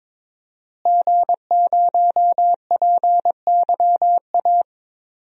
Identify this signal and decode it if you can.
Morse code, heard as a keyed tone, about 22 wpm, 700 Hz. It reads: G0PYA
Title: DXCC-100 Call Signs - 5 Characters - Repeated 3x - 22wpm